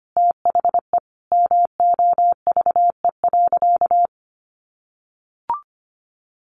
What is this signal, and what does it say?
Morse code, heard as a keyed tone, about 25 wpm, 700 Hz. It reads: THE MOVE.
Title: Sentences from Top 1400 Words 25wpm